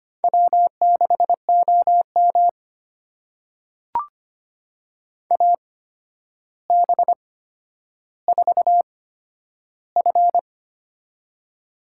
Morse code, heard as a keyed tone, about 25 wpm, spaced as for 11 wpm, 700 Hz. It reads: W6OM AB4F
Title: USA Call Signs - 4 Characters - 8x Character Spacing - 25wpm